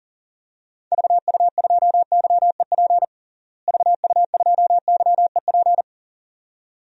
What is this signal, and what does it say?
Morse code, heard as a keyed tone, about 37 wpm, 700 Hz. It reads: VU2YEP VU2YEP
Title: DXCC-100 Call Signs - 6 Characters - Repeated 2x - 40wpm